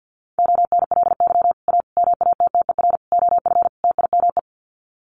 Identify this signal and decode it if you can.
Morse code, heard as a keyed tone, about 50 wpm, 700 Hz. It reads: ONLY A MATTER OF TIME
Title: Sets of 5 Words Encore 50wpm